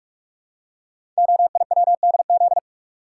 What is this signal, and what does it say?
Morse code, heard as a keyed tone, about 44 wpm, 700 Hz. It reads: OIWDZ
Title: Code Groups - 5 Letters - 45wpm